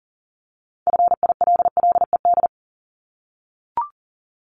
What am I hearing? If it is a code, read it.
Morse code, heard as a keyed tone, about 42 wpm, 700 Hz. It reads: FILLED
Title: Multi-letter Phonemes 40wpm